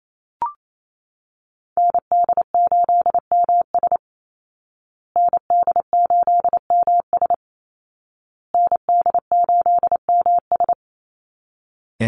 Morse code, heard as a keyed tone, about 28 wpm, 700 Hz. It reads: ND8MS ND8MS ND8MS
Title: Call Signs Repeated 2x - 28wpm